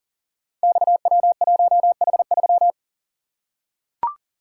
Morse code, heard as a keyed tone, about 42 wpm, 700 Hz. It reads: XW1H3
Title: Code Groups - 5 Letter:Numbers - 40wpm